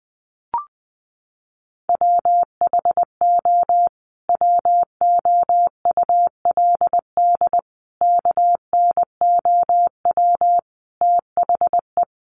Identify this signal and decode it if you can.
Morse code, heard as a keyed tone, about 20 wpm, 700 Hz. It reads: WHO WOULD KNOW THE
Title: Sentences from Top 800 Words 20wpm